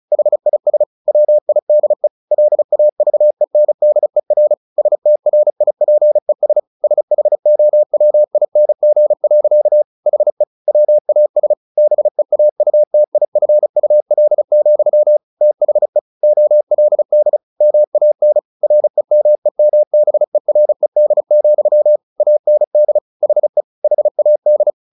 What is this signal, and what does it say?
Morse code, heard as a keyed tone, about 34 wpm, 600 Hz. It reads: HIS WIDE LAVENDER STRIPES SHOWING. HE WAS BEAUTIFUL, THE OLD MAN REMEMBERED, AND HE HAD
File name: the old man and the sea / the old man and the sea 1 35wpm